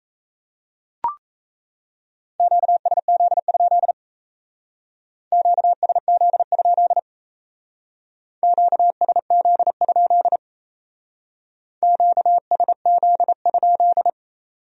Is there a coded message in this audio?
Yes: Morse code from 2.4 to 14.1 s, heard as a keyed tone, about 37 wpm, 700 Hz. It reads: QSZ? QSZ? QSZ? QSZ?